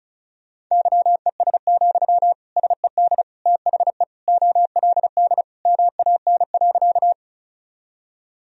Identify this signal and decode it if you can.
Morse code, heard as a keyed tone, about 36 wpm, 700 Hz. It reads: YES, SED THE OLD MAN.
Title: Sentences from Top 500 Words and Abbreviations 35wpm